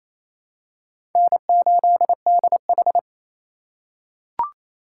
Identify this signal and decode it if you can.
Morse code, heard as a keyed tone, about 28 wpm, 700 Hz. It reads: N8DH
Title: USA Call Signs - 3x Character Spacing - 28wpm